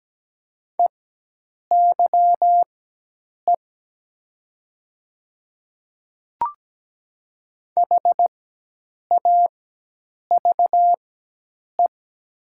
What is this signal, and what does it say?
Morse code, heard as a keyed tone, about 17 wpm, spaced as for 8 wpm, 690 Hz. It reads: EYE HAVE
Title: 4x Character Spacing - Top 300 Words - 17wpm